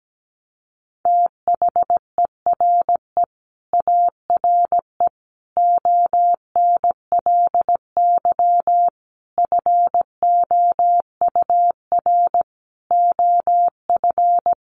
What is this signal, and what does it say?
Morse code, heard as a keyed tone, about 17 wpm, 700 Hz. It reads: THERE ARE ONLY FOUR OF
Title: Sentences from Top 300 Words 17wpm